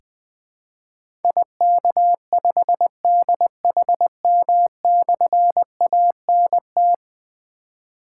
Morse code, heard as a keyed tone, about 20 wpm, 700 Hz. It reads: IK5DHM/ANT
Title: Difficult Call Signs 20wpm